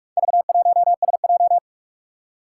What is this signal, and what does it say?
Morse code, heard as a keyed tone, about 46 wpm, 700 Hz. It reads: V1SJ
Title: Four Challenging Letter-Number Combinations 45wpm